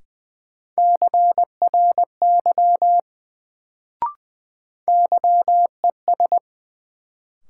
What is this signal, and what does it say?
Morse code, heard as a keyed tone, about 21 wpm, 700 Hz. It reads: CRY YES
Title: Sentences from Top 500 Words - New Word Drill - 20wpm